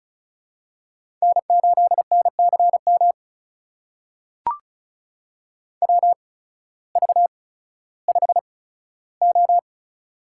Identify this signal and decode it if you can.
Morse code, heard as a keyed tone, about 36 wpm, spaced as for 16 wpm, 700 Hz. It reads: N8NCM WV5O
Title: USA Call Signs - 8x Character Spacing - 35wpm